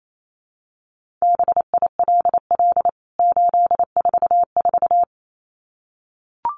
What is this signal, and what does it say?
Morse code, heard as a keyed tone, about 28 wpm, 700 Hz. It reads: BILL 844
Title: CWT Exchange 28wpm